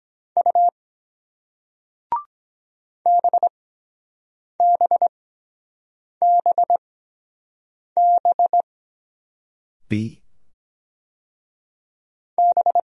Morse code, heard as a keyed tone, about 23 wpm, spaced as for 7 wpm, 700 Hz. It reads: U BBBB B